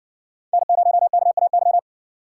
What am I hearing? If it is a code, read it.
Morse code, heard as a keyed tone, about 56 wpm, 700 Hz. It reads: N0KAY